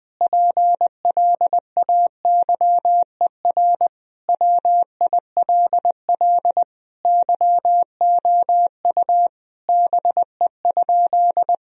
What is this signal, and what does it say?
Morse code, heard as a keyed tone, about 20 wpm, 700 Hz. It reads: PLAYER WILL YOU BE?
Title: Sentences from Top 1000 Words - Review Part 4 - 20wpm